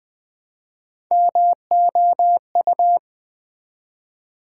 Morse code, heard as a keyed tone, about 20 wpm, 700 Hz. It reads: MOU
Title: Build A Word - 1 Letter At A Time - 5-Letter Words - 20wpm